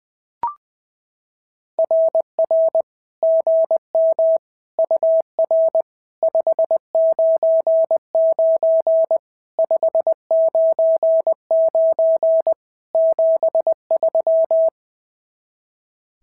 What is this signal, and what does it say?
Morse code, heard as a keyed tone, about 18 wpm, 650 Hz. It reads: RR GM UR 599 599 73
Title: SOTA QSO Practice 20wpm